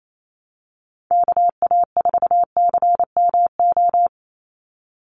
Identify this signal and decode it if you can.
Morse code, heard as a keyed tone, about 28 wpm, 700 Hz. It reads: KA4CMO
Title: USA Call Signs 28wpm